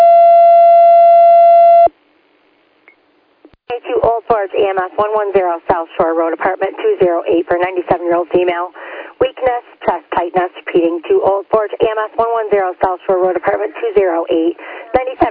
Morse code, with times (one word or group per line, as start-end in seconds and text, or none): none